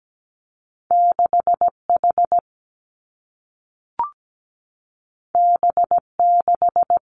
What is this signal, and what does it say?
Morse code, heard as a keyed tone, about 17 wpm, 700 Hz. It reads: 6H B6
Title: Two Challenging Letter-Number Combinations 17wpm